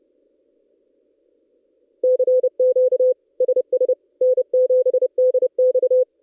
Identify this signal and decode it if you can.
Morse code, heard as a keyed tone, about 29 wpm, 510 Hz. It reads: CQ SS N7DX
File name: contests / ss-cw-k5tr-op-n5rz -1700